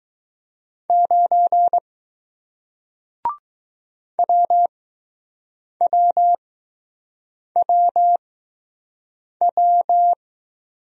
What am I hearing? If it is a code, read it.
Morse code, heard as a keyed tone, about 20 wpm, spaced as for 6 wpm, 700 Hz. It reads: 9 WWWW